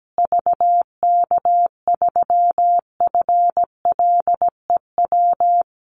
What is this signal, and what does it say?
Morse code, heard as a keyed tone, about 17 wpm, 700 Hz. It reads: VK3FLEW
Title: USA Call Signs Repeated 3x 17wpm